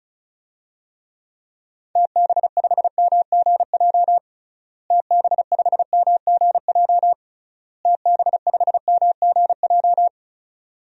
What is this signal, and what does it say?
Morse code, heard as a keyed tone, about 36 wpm, 700 Hz. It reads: TB5MGJ TB5MGJ TB5MGJ
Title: DXCC-100 Call Signs - 6 Characters - Repeated 3x - 35wpm